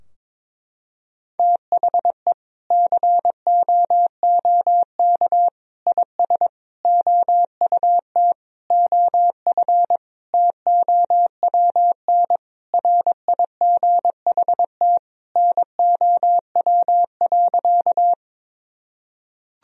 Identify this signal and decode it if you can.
Morse code, heard as a keyed tone, about 22 wpm, 700 Hz. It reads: THE COOK IS OUT OF TOWN RIGHT NOW.
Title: Sentences from Top 1000 Words - Review Part 4 - 22wpm